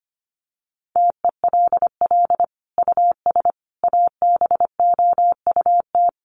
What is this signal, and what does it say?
Morse code, heard as a keyed tone, about 25 wpm, 700 Hz. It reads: TELL US ABOUT
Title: Sentences from Top 1100 Words 25wpm